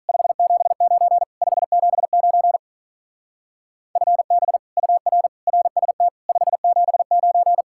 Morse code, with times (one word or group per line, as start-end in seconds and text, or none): 0.1-1.2 579
1.4-2.6 579
4.0-4.6 FB
4.8-5.3 UR
5.5-6.1 RST
6.3-7.6 579